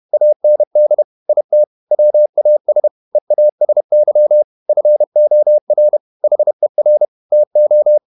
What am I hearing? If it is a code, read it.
Morse code, heard as a keyed tone, about 31 wpm, 600 Hz. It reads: AND IT WAS EASY FOR HER TO
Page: Never 31wpm | CW med Gnister